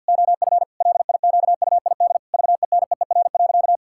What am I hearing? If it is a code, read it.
Morse code, heard as a keyed tone, about 51 wpm, 700 Hz. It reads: OF LIQUID VENEER.
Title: Pangram Challenge v2 - 50wpm